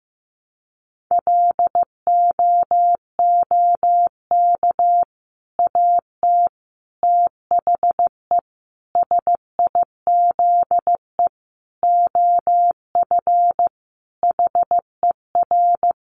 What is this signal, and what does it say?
Morse code, heard as a keyed tone, about 15 wpm, 700 Hz. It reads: LOOK AT THE SIZE OF HER
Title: Sentences from Top 1200 Words 15wpm